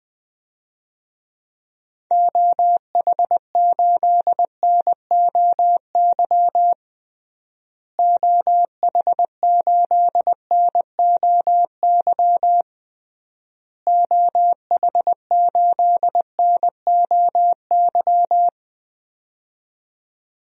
Morse code, heard as a keyed tone, about 20 wpm, 700 Hz. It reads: OH8NOY OH8NOY OH8NOY
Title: DXCC-100 Call Signs - 6 Characters - Repeated 3x - 20wpm